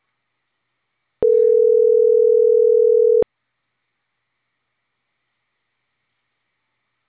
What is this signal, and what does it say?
Ringback tone (North America), 440 + 480 Hz.